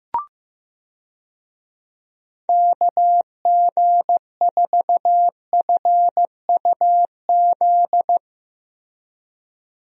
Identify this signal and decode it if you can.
Morse code, heard as a keyed tone, about 15 wpm, 700 Hz. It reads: KG4FUZ
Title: USA Call Signs - 6 Characters - Repeated 2x - 15wpm